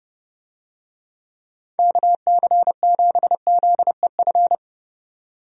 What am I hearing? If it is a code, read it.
Morse code, heard as a keyed tone, about 30 wpm, 700 Hz. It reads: KC7ZEF
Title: USA Call Signs - 6 Characters - 30wpm